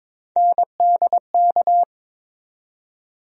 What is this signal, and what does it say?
Morse code, heard as a keyed tone, about 22 wpm, 700 Hz. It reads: NDK
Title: USA Call Sign Postfixes - 3 Letters 22wpm